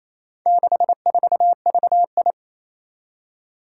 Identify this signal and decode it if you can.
Morse code, heard as a keyed tone, about 27 wpm, 700 Hz. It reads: 64VI
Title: Four Challenging Letter-Number Combinations 28wpm